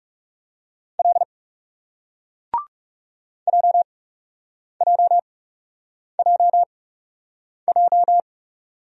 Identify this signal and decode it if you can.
Morse code, heard as a keyed tone, about 42 wpm, spaced as for 8 wpm, 700 Hz. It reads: L JJJJ